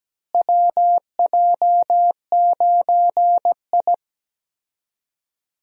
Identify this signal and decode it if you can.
Morse code, heard as a keyed tone, about 17 wpm, 700 Hz. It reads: WJ9I
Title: USA Call Signs - 4 Characters - Repeated 3x - 17wpm